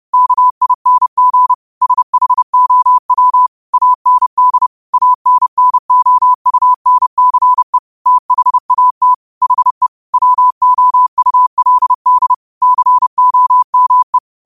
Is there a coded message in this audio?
Yes: Morse code from 0.1 to 14.2 s, heard as a keyed tone, about 30 wpm, 1000 Hz. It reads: KING SHOW AND ANNOUNCE THAT HE WOULD COME